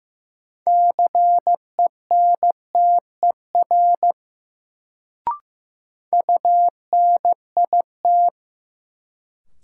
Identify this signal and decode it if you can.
Morse code, heard as a keyed tone, about 15 wpm, 700 Hz. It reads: CENTER UNIT
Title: Sentences from Top 1000 Words - Review Part 2 - New Word Drill - 15wpm